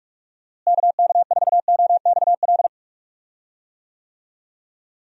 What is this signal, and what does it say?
Morse code, heard as a keyed tone, about 48 wpm, 700 Hz. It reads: KK4OXL
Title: USA Call Signs - 6 Characters - Repeated 2x - 45wpm